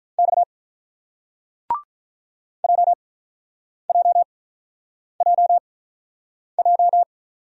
Morse code, heard as a keyed tone, about 44 wpm, spaced as for 8 wpm, 700 Hz. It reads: X JJJJ